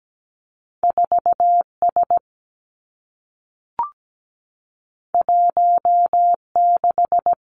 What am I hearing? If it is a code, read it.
Morse code, heard as a keyed tone, about 17 wpm, 700 Hz. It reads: 4S 16
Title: Two Challenging Letter-Number Combinations 17wpm